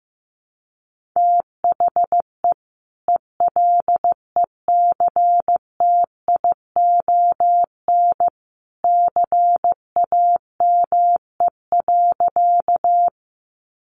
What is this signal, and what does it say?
Morse code, heard as a keyed tone, about 15 wpm, 700 Hz. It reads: THE ELECTION CAME.
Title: Sentences from Top 1500 Words 15wpm